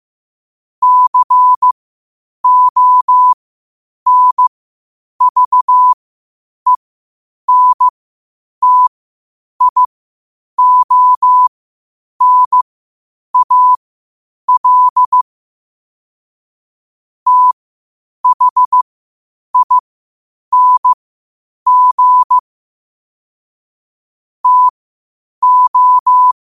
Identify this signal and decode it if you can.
Morse code, heard as a keyed tone, about 15 wpm, spaced as for 8 wpm, 1000 Hz. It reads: CONVENTIONAL THING TO